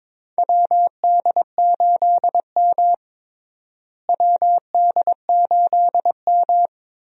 Morse code, heard as a keyed tone, about 22 wpm, 700 Hz. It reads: WD8M WD8M
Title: USA Call Signs - 4 Characters - Repeated 3x - 22wpm